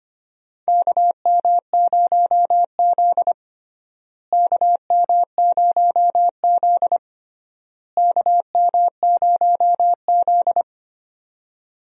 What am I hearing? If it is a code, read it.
Morse code, heard as a keyed tone, about 25 wpm, 700 Hz. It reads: KM0Z KM0Z KM0Z
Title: USA Call Signs - 4 Characters - Repeated 3x - 25wpm